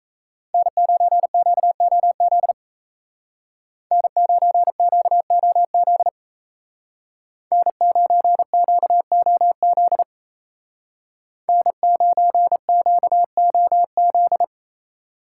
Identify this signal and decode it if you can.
Morse code, heard as a keyed tone, about 36 wpm, 700 Hz. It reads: N9QOZ N9QOZ N9QOZ N9QOZ